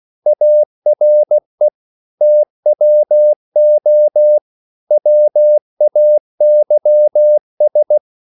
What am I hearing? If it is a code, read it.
Morse code, heard as a keyed tone, about 16 wpm, 600 Hz. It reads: ARE TWO WAYS
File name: CW / Practice / EnglishQuotations0216